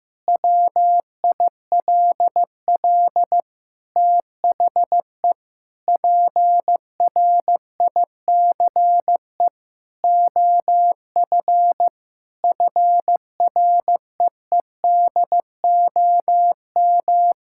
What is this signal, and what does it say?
Morse code, heard as a keyed tone, about 16 wpm, 700 Hz. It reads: WILL THE PRICE OF FREEDOM
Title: Sentences from Top 1000 Words - Review Part 5 - 15wpm